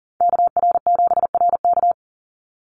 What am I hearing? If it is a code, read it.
Morse code, heard as a keyed tone, about 39 wpm, 700 Hz. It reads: KR7RK
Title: NA SOTA Activator Callsigns 40wpm